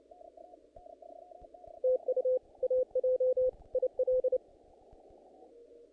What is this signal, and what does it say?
Morse code, heard as a keyed tone, about 29 wpm, 510 Hz. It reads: TU AJ IL
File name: contests / naqp-cw-k5tr-op-k5pi- -1900